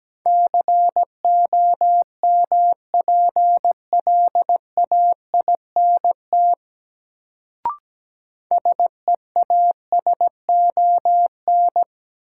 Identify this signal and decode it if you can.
Morse code, heard as a keyed tone, about 17 wpm, 700 Hz. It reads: COMPLAINT SEASON
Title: Sentences from Top 1400 Words - New Word Drill - 17wpm